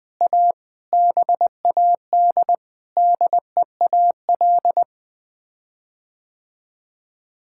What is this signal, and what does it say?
Morse code, heard as a keyed tone, about 20 wpm, 700 Hz. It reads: A BAD DEAL
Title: Sets of 9 Words - 20wpm